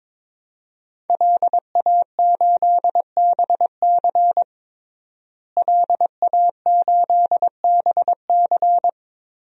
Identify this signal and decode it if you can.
Morse code, heard as a keyed tone, about 21 wpm, 700 Hz. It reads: LA8BC LA8BC
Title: DXCC-100 Call Signs - 5 Characters - Repeated 2x - 22wpm